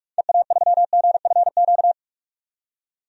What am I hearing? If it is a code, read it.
Morse code, heard as a keyed tone, about 43 wpm, 700 Hz. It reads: EA3GFQ
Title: DXCC-100 Call Signs - 6 Characters - 45wpm